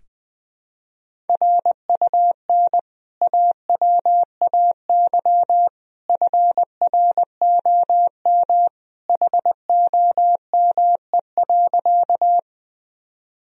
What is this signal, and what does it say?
Morse code, heard as a keyed tone, about 20 wpm, 700 Hz. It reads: RUN AWAY FROM HOME.
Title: Sentences from Top 1000 Words - Review Part 1 - 20wpm